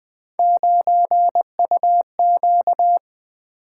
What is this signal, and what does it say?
Morse code, heard as a keyed tone, about 20 wpm, 700 Hz. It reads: 9UQ
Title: Three Letter-Number Combinations - 20wpm